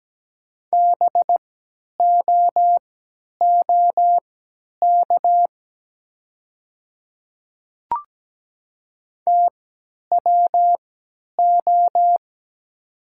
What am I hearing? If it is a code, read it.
Morse code, heard as a keyed tone, about 17 wpm, spaced as for 10 wpm, 700 Hz. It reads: BOOK TWO
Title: 3x Character Spacing - Top 300 Words - 17wpm